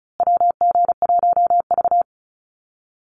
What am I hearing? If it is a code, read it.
Morse code, heard as a keyed tone, about 34 wpm, 700 Hz. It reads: WG1V
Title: USA Call Signs Repeated 3x 35wpm